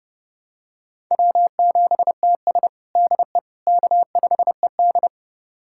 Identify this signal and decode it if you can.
Morse code, heard as a keyed tone, about 32 wpm, 700 Hz. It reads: W7TS DE K5ED
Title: USA Callsign DE Callsign - 4 Characters - 30wpm